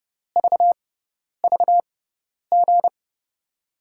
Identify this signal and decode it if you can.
Morse code, heard as a keyed tone, about 30 wpm, spaced as for 10 wpm, 700 Hz. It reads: VVG
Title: USA Call Signs - 6 Characters - 6x Character Spacing - 30wpm